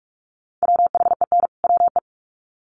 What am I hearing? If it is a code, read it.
Morse code, heard as a keyed tone, about 44 wpm, 700 Hz. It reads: WHEN WE
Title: Sets of 2 Words 45wpm